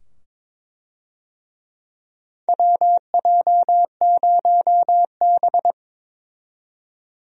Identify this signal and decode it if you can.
Morse code, heard as a keyed tone, about 22 wpm, 700 Hz. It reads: WJ0B
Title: USA Call Signs - 4 Characters - 22wpm